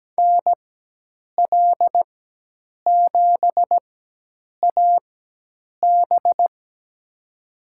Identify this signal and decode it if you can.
Morse code, heard as a keyed tone, about 17 wpm, spaced as for 8 wpm, 700 Hz. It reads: NL7AB
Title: USA Call Signs - 4x Character Spacing - 17wpm